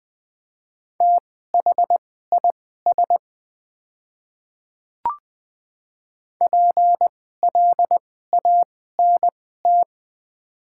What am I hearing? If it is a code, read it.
Morse code, heard as a keyed tone, about 20 wpm, spaced as for 14 wpm, 700 Hz. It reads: THIS PLANT
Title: 2x Character Spacing - Top 500 Words - 20wpm